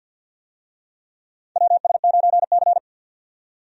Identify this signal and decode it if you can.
Morse code, heard as a keyed tone, about 49 wpm, 700 Hz. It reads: WS9C
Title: USA Call Signs - 4 Characters - 50wpm